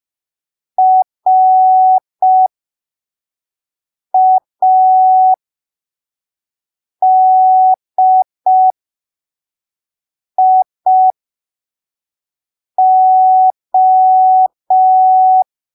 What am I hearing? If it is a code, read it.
Morse code, heard as a keyed tone, about 5 wpm, spaced as for 3 wpm, 750 Hz. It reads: RADIO
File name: arquivos / texto / texto 14 05ppm